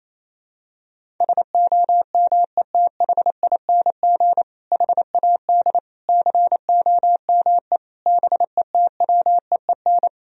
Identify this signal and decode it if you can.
Morse code, heard as a keyed tone, about 28 wpm, 700 Hz. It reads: SOMETHING HAD COME BETWEEN
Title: Sentences from Top 1000 Words - Review Part 1 - 28wpm